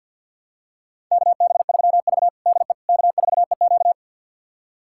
Morse code, heard as a keyed tone, about 49 wpm, 700 Hz. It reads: KB3V DE K4EQ
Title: USA Callsign DE Callsign - 4 Characters - 50wpm